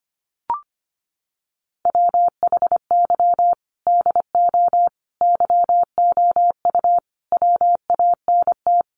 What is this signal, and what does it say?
Morse code, heard as a keyed tone, about 25 wpm, 700 Hz. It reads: WHY DO YOU WANT
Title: Sets of 5 Words Encore2 25wpm